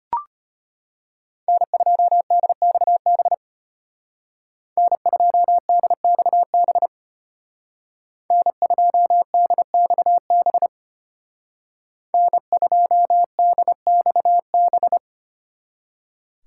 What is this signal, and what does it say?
Morse code, heard as a keyed tone, about 33 wpm, 700 Hz. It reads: N2DXB N2DXB N2DXB N2DXB